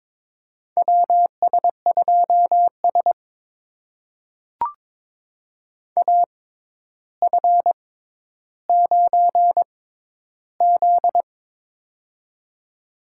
Morse code, heard as a keyed tone, about 22 wpm, spaced as for 11 wpm, 700 Hz. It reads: WS2S AF9Z
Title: USA Call Signs - 4 Characters - 6x Character Spacing - 22wpm